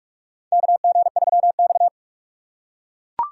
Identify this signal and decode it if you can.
Morse code, heard as a keyed tone, about 42 wpm, 690 Hz. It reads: KG3X
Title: USA Call Signs - 4 Characters - 4x Character Spacing - 45wpm